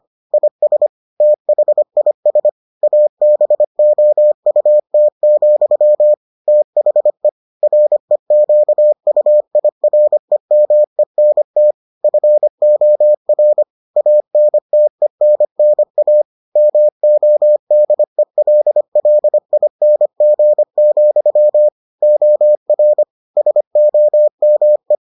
Morse code, heard as a keyed tone, about 25 wpm, 600 Hz. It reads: IS THIS ABOUT, THE REQUIREMENT FOR ANTENNA MODELLING, OR SOME